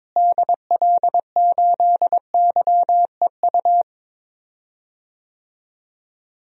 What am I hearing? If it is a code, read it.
Morse code, heard as a keyed tone, about 23 wpm, 700 Hz. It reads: DL8YEU
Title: DXCC-100 Call Signs - 6 Characters - Repeated 2x - 22wpm